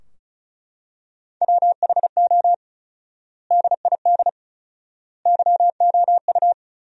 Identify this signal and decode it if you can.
Morse code, heard as a keyed tone, about 34 wpm, 700 Hz. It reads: WHO DID YOU
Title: 4x Word-Spacing - Sentences from Top 300 Words - 35wpm